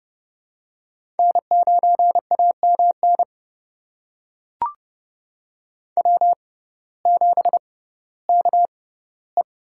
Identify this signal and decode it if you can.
Morse code, heard as a keyed tone, about 30 wpm, spaced as for 16 wpm, 700 Hz. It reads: N9AMN W7KE
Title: USA Call Signs - 5 Characters - 6x Character Spacing - 30wpm